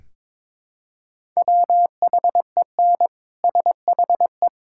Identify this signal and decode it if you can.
Morse code, heard as a keyed tone, about 22 wpm, 700 Hz. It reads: WHEN SHE
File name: Build A Phrase - 2-Word Phrases - 22wpm